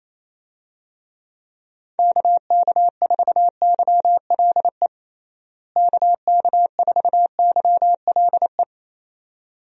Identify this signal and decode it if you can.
Morse code, heard as a keyed tone, about 28 wpm, 700 Hz. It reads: KK4YLE KK4YLE
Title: USA Call Signs - 6 Characters - Repeated 2x - 28wpm